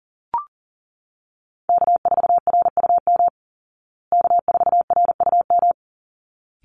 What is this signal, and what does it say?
Morse code, heard as a keyed tone, about 40 wpm, 700 Hz. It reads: K4RUM K4RUM
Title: Call Signs Repeated 3x - 40wpm